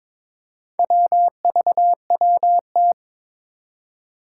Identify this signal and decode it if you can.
Morse code, heard as a keyed tone, about 22 wpm, 700 Hz. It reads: WVWT
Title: Four Challenging Letter-Number Combinations 22wpm